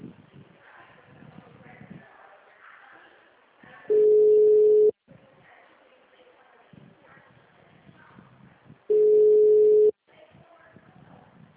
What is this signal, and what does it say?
Ringback tone (Europe, 425 Hz).
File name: out- -2001